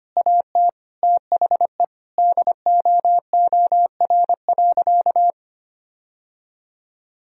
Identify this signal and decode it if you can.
Morse code, heard as a keyed tone, about 25 wpm, 700 Hz. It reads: AT THE DOOR.
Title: Sentences from Top 1000 Words - Review Part 2 - 25wpm